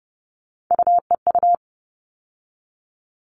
Morse code, heard as a keyed tone, about 30 wpm, 700 Hz. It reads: UEU
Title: All Cut Numbers - 3-Digit 30wpm